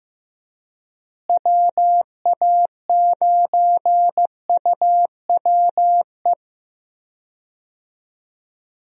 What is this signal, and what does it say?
Morse code, heard as a keyed tone, about 16 wpm, 700 Hz. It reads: WA9UWE
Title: USA Call Signs - 6 Characters - Repeated 3x - 15wpm